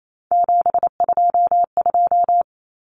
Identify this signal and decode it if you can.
Morse code, heard as a keyed tone, about 28 wpm, 700 Hz. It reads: 722
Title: Numbers - 3-Digit 28wpm